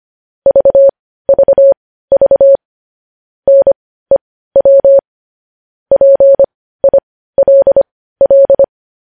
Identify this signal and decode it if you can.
Morse code, heard as a keyed tone, about 25 wpm, spaced as for 15 wpm, 550 Hz. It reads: VVV NEW PILL
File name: podcast / News headlines-1x-25x15